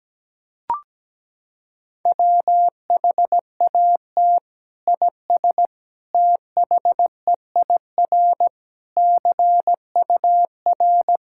Morse code, heard as a keyed tone, about 17 wpm, 700 Hz. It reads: WHAT IS THEIR CUR
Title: Sentences from Top 1000 Words - Review Part 6 - 17wpm